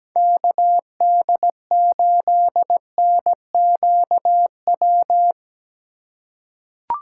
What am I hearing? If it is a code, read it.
Morse code, heard as a keyed tone, about 17 wpm, 700 Hz. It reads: KD8NQW
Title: USA Call Signs - 6 Characters - 3x Character Spacing - 17wpm